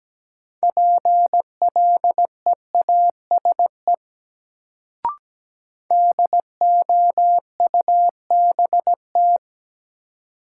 Morse code, heard as a keyed tone, about 17 wpm, 700 Hz. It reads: PLEASE DOUBT
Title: Sentences from Top 1500 Words - New Word Drill Alt - 17wpm